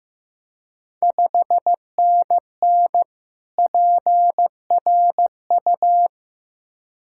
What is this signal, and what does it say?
Morse code, heard as a keyed tone, about 15 wpm, 690 Hz. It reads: 5NN PRU
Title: Ontario QSO Party Exchange 15wpm